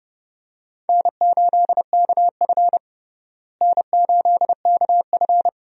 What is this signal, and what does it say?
Morse code, heard as a keyed tone, about 32 wpm, 700 Hz. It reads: N8KF N8KF
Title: USA Call Signs - 4 Characters - Repeated 2x - 30wpm